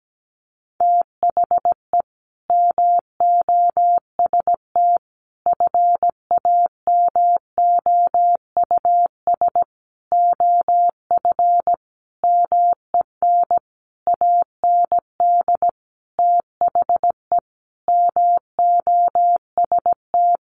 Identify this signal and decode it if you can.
Morse code, heard as a keyed tone, about 17 wpm, 700 Hz. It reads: THE MOST FAMOUS OF MEN AND THE MOST
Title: Sentences from Top 1500 Words 17wpm